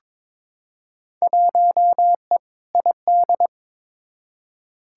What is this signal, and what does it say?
Morse code, heard as a keyed tone, about 23 wpm, 700 Hz. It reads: 1E ID
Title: Field Day Exchanges - 22wpm